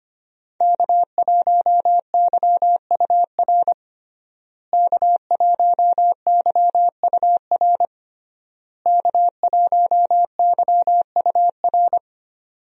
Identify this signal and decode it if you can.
Morse code, heard as a keyed tone, about 25 wpm, 700 Hz. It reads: K1YUR K1YUR K1YUR
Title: USA Call Signs - 5 Characters - Repeated 3x - 25wpm